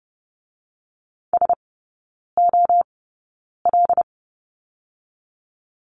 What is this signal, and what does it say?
Morse code, heard as a keyed tone, about 30 wpm, spaced as for 9 wpm, 700 Hz. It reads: SOL
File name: Lesson 17e - Letter u - Three Characters 30wpm